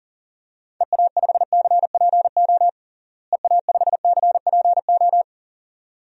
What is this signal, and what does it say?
Morse code, heard as a keyed tone, about 40 wpm, 700 Hz. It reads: EA5CPO EA5CPO
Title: DXCC-100 Call Signs - 6 Characters - Repeated 2x - 40wpm